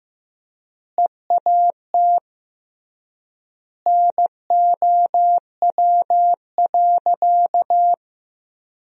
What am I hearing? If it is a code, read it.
Morse code, heard as a keyed tone, about 15 wpm, 700 Hz. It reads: EAT NOW.
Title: 3x Word-Spacing - Sentences from Top 300 Words - 15wpm